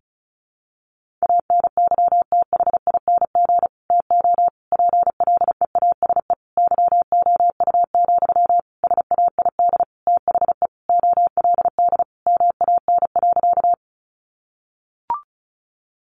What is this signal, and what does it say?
Morse code, heard as a keyed tone, about 34 wpm, 700 Hz. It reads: ANYTHING TO PLEASE YOU, SAID THE OLD MAN.
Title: Sentences from Top 1500 Words 35wpm